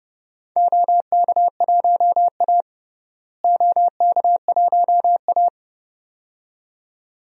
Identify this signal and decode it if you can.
Morse code, heard as a keyed tone, about 30 wpm, 700 Hz. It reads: OK1A OK1A
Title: DXCC-100 Call Signs - 4 Characters - Repeated 2x - 30wpm